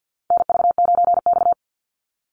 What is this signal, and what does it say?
Morse code, heard as a keyed tone, about 50 wpm, 700 Hz. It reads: NV9X
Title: Call Signs Repeated 3x - 50wpm